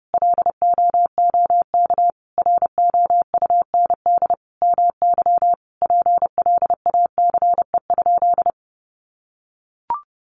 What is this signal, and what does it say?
Morse code, heard as a keyed tone, about 30 wpm, 700 Hz. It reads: LOOK ROUND MY PLACE?
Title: Sentences from Top 200 Words 30wpm